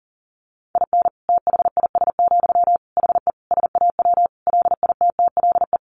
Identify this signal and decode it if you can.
Morse code, heard as a keyed tone, about 40 wpm, 700 Hz. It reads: IN THIS, HE SAW LITTLE
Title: Sentences from Top 1500 Words 40wpm